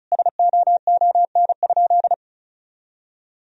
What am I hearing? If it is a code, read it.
Morse code, heard as a keyed tone, about 36 wpm, 700 Hz. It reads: SOON?
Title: Sentences from Top 1000 Words - Review Part 4 - 35wpm